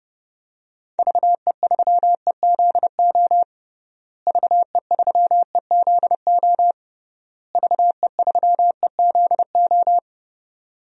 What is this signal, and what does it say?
Morse code, heard as a keyed tone, about 30 wpm, 700 Hz. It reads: VE3EZO VE3EZO VE3EZO
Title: DXCC-100 Call Signs - 6 Characters - Repeated 3x - 30wpm